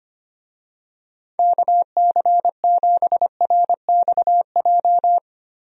Morse code, heard as a keyed tone, about 25 wpm, 700 Hz. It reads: KC7RXJ
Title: USA Call Signs - 6 Characters - Repeated 3x - 25wpm